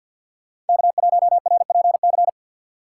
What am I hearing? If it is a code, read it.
Morse code, heard as a keyed tone, about 46 wpm, 700 Hz. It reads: K1RPC
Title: USA Call Signs - 2x Character Spacing - 50wpm